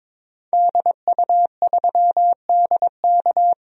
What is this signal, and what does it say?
Morse code, heard as a keyed tone, about 22 wpm, 700 Hz. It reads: DU3DK
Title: DXCC-100 Call Signs - 5 Characters - Repeated 3x - 22wpm